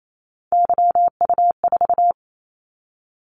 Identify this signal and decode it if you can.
Morse code, heard as a keyed tone, about 28 wpm, 700 Hz. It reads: YU4
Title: Callsign Prefixes for Top 25 DXCC - 28wpm